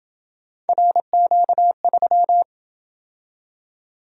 Morse code, heard as a keyed tone, about 26 wpm, 700 Hz. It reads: RQ3